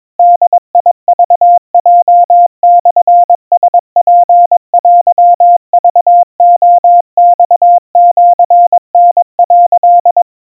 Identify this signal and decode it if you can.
Morse code (Japanese Wabun code), heard as a keyed tone, about 22 wpm, 700 Hz. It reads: ボクヲモラツテクレマシタ」
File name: 80 cpm 700 Hz Crispy